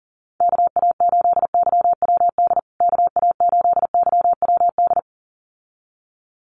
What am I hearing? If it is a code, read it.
Morse code, heard as a keyed tone, about 40 wpm, 700 Hz. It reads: KA8YWD KA8YWD
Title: USA Call Signs Repeated 2x 40wpm